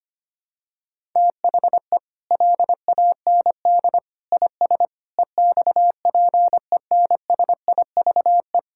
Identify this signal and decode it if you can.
Morse code, heard as a keyed tone, about 25 wpm, 700 Hz. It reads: THE LAND IS EXPENSIVE
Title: Sentences from Top 1000 Words - Review Part 6 - 25wpm